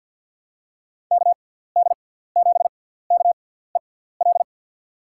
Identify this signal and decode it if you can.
Morse code, heard as a keyed tone, about 50 wpm, spaced as for 17 wpm, 700 Hz. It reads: KD7KEL
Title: USA Call Signs - 6 Characters - 6x Character Spacing - 50wpm